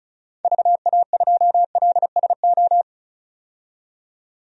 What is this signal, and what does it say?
Morse code, heard as a keyed tone, about 34 wpm, 700 Hz. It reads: VA2LSO
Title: DXCC-100 Call Signs - 6 Characters - 35wpm